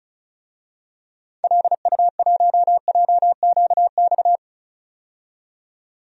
Morse code, heard as a keyed tone, about 34 wpm, 700 Hz. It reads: LU1JQX
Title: DXCC-100 Call Signs - 6 Characters - 35wpm